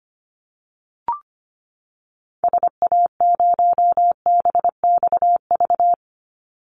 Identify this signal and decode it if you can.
Morse code, heard as a keyed tone, about 25 wpm, 700 Hz. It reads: SA0BXV
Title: Call Signs Repeated 3x - 25wpm